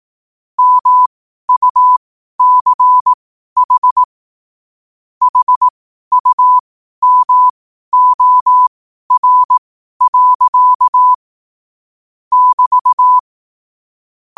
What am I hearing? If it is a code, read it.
Morse code, heard as a keyed tone, about 18 wpm, spaced as for 12 wpm, 1000 Hz. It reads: MUCH HUMOR. <BT>
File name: qotdfiles / 12wpm